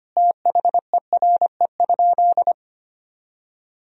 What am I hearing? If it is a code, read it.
Morse code, heard as a keyed tone, about 25 wpm, 700 Hz. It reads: THERE?
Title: Sentences from Top 1000 Words - Review Part 1 - 25wpm